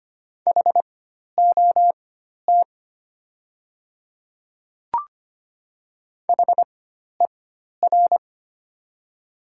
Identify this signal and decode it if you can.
Morse code, heard as a keyed tone, about 25 wpm, spaced as for 12 wpm, 700 Hz. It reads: HOT HER
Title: 4x Character Spacing - Top 100 Words - 25wpm